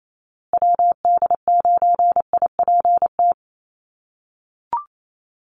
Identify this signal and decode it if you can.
Morse code, heard as a keyed tone, about 28 wpm, 700 Hz. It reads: WD9IPT
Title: USA Call Signs Repeated 3x 28wpm